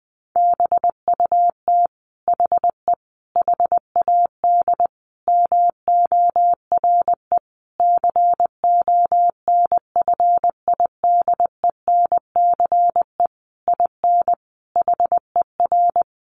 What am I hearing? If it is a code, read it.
Morse code, heard as a keyed tone, about 20 wpm, 700 Hz. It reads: BUT HE HAD MORE CONFIDENCE IN HER
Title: Sentences from Top 1500 Words 20wpm